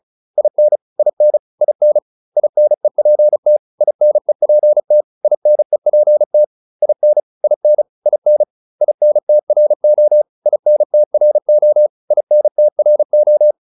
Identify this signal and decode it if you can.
Morse code, heard as a keyed tone, about 34 wpm, 600 Hz. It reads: IN IN IN INEPT INEPT INEPT IN IN IN INTRO INTRO INTRO
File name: IN5 35